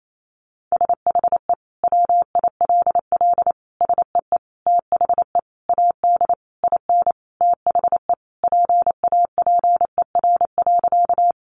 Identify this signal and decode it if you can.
Morse code, heard as a keyed tone, about 28 wpm, 700 Hz. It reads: SHE WILL SEE THE AD IN THE PAPER.
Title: Sentences from Top 1300 Words 28wpm